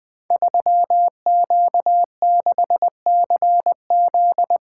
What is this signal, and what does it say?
Morse code, heard as a keyed tone, about 20 wpm, 700 Hz. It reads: 3Q6CZ
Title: Code Groups - 5 Letter:Numbers - 20wpm